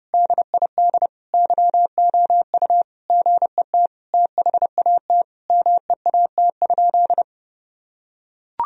Morse code, heard as a keyed tone, about 30 wpm, 700 Hz. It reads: DID YOU GET THAT MEAT?
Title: Sentences from Top 1000 Words - Review Part 3 - 30wpm